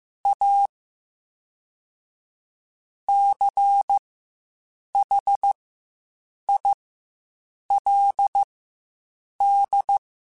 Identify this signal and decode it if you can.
Morse code, heard as a keyed tone, about 15 wpm, spaced as for 7 wpm, 800 Hz. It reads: A CHILD
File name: Quotes Spaced / Quotes Spaced-mp3s / ChildAsleep sp @10